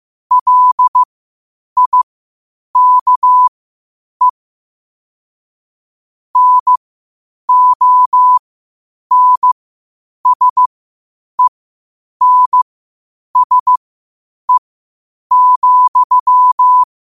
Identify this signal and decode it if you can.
Morse code, heard as a keyed tone, about 15 wpm, spaced as for 8 wpm, 1000 Hz. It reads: LIKE NONSENSE,